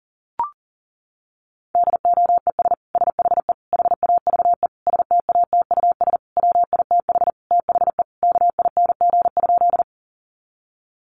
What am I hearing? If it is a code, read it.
Morse code, heard as a keyed tone, about 40 wpm, 700 Hz. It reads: DOES SHE HAVE STATUS WITH THE KING?
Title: Sentences from Top 1400 Words 40wpm